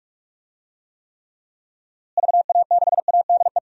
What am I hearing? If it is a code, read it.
Morse code, heard as a keyed tone, about 42 wpm, 700 Hz. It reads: VA6ADE
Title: DXCC-100 Call Signs - 6 Characters - Repeated 2x - 45wpm